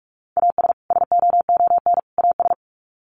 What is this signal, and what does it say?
Morse code, heard as a keyed tone, about 44 wpm, 700 Hz. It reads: AS SOON AS
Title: Sets of 3 Words 45wpm